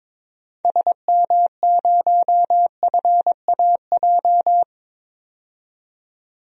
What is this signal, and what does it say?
Morse code, heard as a keyed tone, about 22 wpm, 700 Hz. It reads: SM0FAJ
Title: DXCC-100 Call Signs - 6 Characters - 22wpm